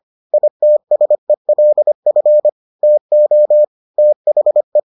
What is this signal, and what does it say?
Morse code, heard as a keyed tone, about 25 wpm, 600 Hz. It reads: ITSELF TO THE